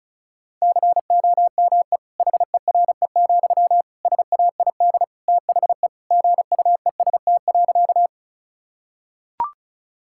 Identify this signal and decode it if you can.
Morse code, heard as a keyed tone, about 36 wpm, 700 Hz. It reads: COME HERE, SAID THE GUEST.
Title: Sentences from Top 1600 Words 35wpm